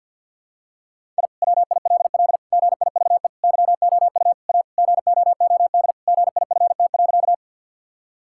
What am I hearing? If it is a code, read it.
Morse code, heard as a keyed tone, about 50 wpm, 700 Hz. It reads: I WILL GIVE YOU A GOOD GIFT.
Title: Sentences from Top 1000 Words - Review Part 6 - 50wpm